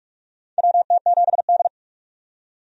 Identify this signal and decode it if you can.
Morse code, heard as a keyed tone, about 46 wpm, 700 Hz. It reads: WT7D
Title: USA Call Signs - 4 Characters - 45wpm